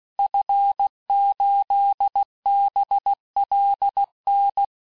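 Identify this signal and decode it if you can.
Morse code, heard as a keyed tone, about 16 wpm, 800 Hz. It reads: F8BLN
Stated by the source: Morse course text, end of QSO, sent at 15 words per minute.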